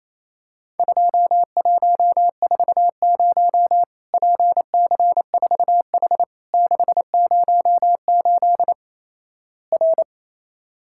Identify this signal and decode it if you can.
Morse code, heard as a keyed tone, about 28 wpm, 700 Hz. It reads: 2140 PC4H 608 R
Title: Worked All Europe DX Contest - QTC Exchange - 28wpm